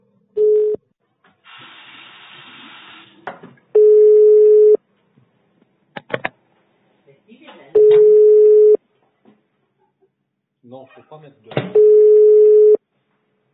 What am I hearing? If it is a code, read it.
Telephone call progress tone (425 Hz, 1 s on, 3 s off), heard as a telephone network's tone, about 420 Hz, with speech. Ringback tone.